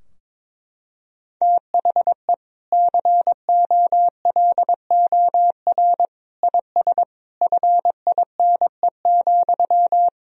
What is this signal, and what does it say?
Morse code, heard as a keyed tone, about 22 wpm, 700 Hz. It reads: THE COLOR IS FINE,
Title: Sentences from Top 1000 Words - Review Part 2 - 22wpm